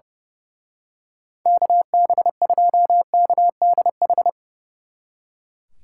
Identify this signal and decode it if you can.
Morse code, heard as a keyed tone, about 28 wpm, 700 Hz. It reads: KB2KDH
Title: Speed Racing - USA 2x3 Call Signs 30wpm